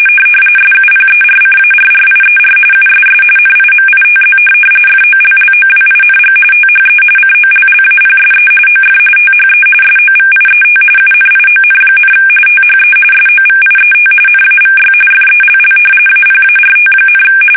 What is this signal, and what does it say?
RTTY, about 45.45 baud. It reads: SUPERCARRIER USS NIMITZ FOLLOWING THE COLLAPSE OF THE SOVIET UNION, THE SOVIET NAVY FELL APART,
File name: fsk850 / usnavy5